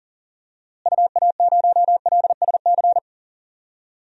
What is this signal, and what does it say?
Morse code, heard as a keyed tone, about 40 wpm, 700 Hz. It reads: UA0LSC
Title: DXCC-100 Call Signs - 6 Characters - Repeated 3x - 40wpm